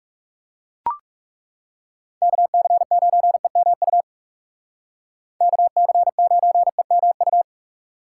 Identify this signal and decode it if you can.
Morse code, heard as a keyed tone, about 38 wpm, 700 Hz. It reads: KC9EMU KC9EMU
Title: Speed Racing - USA 2x3 Call Signs 30wpm